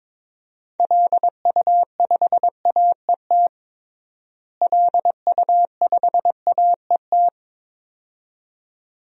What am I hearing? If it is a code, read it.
Morse code, heard as a keyed tone, about 21 wpm, 700 Hz. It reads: LU5AET LU5AET
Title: DXCC-100 Call Signs - 6 Characters - Repeated 2x - 22wpm